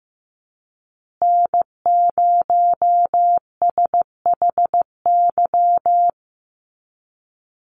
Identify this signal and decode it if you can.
Morse code, heard as a keyed tone, about 15 wpm, 700 Hz. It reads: N0SHY
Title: USA Call Signs Repeated 2x 15wpm